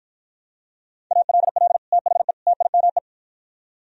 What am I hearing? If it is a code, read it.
Morse code, heard as a keyed tone, about 52 wpm, 700 Hz. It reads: ALL THE TIME